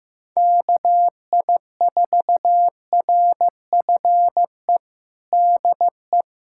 Morse code, heard as a keyed tone, about 15 wpm, 700 Hz. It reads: KI4RFE DE
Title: USA Callsign DE Callsign - 6 Characters - 15wpm